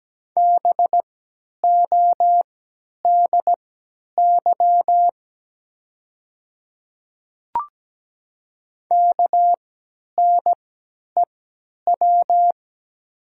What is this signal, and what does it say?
Morse code, heard as a keyed tone, about 17 wpm, spaced as for 10 wpm, 700 Hz. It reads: BODY KNEW
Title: 3x Character Spacing - Top 500 Words - 17wpm